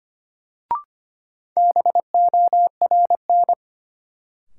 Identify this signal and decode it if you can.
Morse code, heard as a keyed tone, about 25 wpm, 700 Hz. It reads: BORN
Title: Sentences from Top 1000 Words - New Word Drill - 25wpm